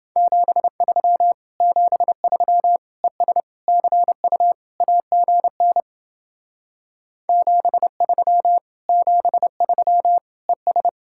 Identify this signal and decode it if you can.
Morse code, heard as a keyed tone, about 29 wpm, 700 Hz. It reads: 73 73 ES CU AGN 73 73 ES